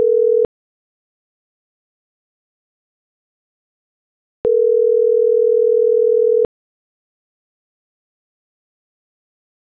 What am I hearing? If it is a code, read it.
Ringback tone (North America), 440 + 480 Hz.
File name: CENTRAL GENERAL / out deenis.chavez